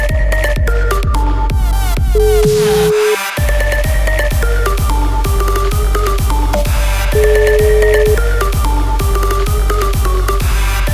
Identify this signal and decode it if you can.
Ringback tone (Europe, 425 Hz), with noise.